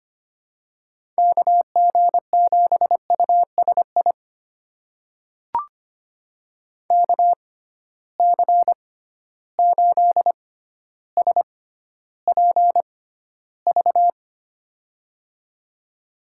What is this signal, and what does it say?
Morse code, heard as a keyed tone, about 25 wpm, spaced as for 13 wpm, 700 Hz. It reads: KG7USI KC8SPV
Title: USA Call Signs - 6 Characters - 6x Character Spacing - 25wpm